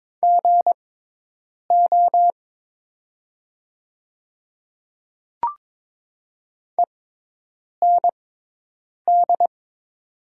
Morse code, heard as a keyed tone, about 21 wpm, spaced as for 8 wpm, 700 Hz. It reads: GO END
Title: 6x Character Spacing - Top 200 Words - 22wpm